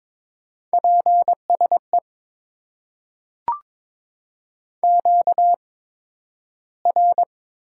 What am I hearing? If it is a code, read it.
Morse code, heard as a keyed tone, about 22 wpm, spaced as for 12 wpm, 700 Hz. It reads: PSE QR
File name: Top 90 QSO Elements - 8x Character Spacing - 22wpm